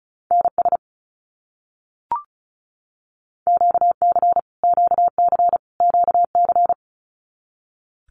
Morse code, heard as a keyed tone, about 34 wpm, 700 Hz. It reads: NS QC QC QC
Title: Canadian Provinces and Territories - Repeated 3x - 35wpm